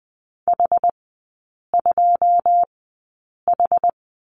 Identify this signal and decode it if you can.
Morse code, heard as a keyed tone, about 20 wpm, spaced as for 8 wpm, 700 Hz. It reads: H2H
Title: 100 Lesson 17e - Letter u - Three Characters 20wpm